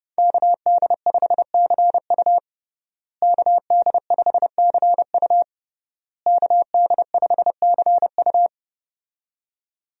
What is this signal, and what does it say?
Morse code, heard as a keyed tone, about 29 wpm, 700 Hz. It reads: KD5CU KD5CU KD5CU
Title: USA Call Signs - 5 Characters - Repeated 3x - 30wpm